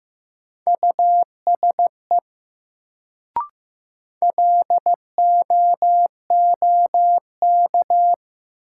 Morse code, heard as a keyed tone, about 15 wpm, 700 Hz. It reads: USE LOOK
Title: Sets of 2 Words - New Word Drill - 15wpm